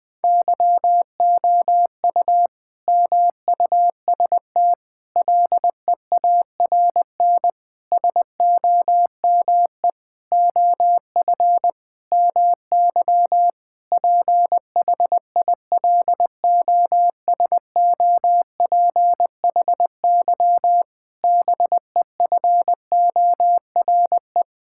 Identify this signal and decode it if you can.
Morse code, heard as a keyed tone, about 20 wpm, 700 Hz. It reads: YOU MUST LEARN SOME OF MY PHILOSOPHY BEFORE
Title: Sentences from Top 1000 Words - Review Part 6 - 20wpm